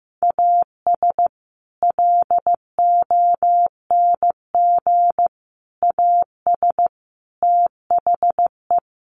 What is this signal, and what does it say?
Morse code, heard as a keyed tone, about 15 wpm, 700 Hz. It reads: AS LONG AS THE
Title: Sets of 4 Words 15wpm